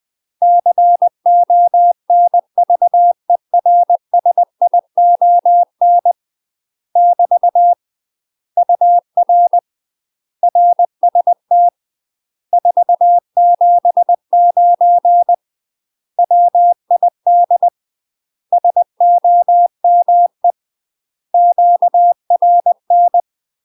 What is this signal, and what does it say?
Morse code, heard as a keyed tone, about 20 wpm, 700 Hz. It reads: CONVERSION <BT> UR RST 479 WID SOME QRN
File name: UK MASTER QSOs TRAINING / 20WPM / UK MASTER QSOs 0009